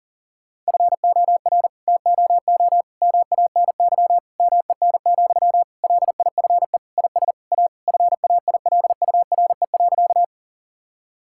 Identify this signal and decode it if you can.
Morse code, heard as a keyed tone, about 40 wpm, 700 Hz. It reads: FOR TOO MANY MEN, LIFE IS A FAILURE.
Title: Sentences from Top 1000 Words - Review Part 6 - 40wpm